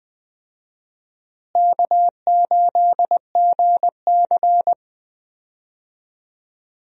Morse code, heard as a keyed tone, about 20 wpm, 700 Hz. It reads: K8GC
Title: USA Call Signs - 4 Characters - Repeated 3x - 20wpm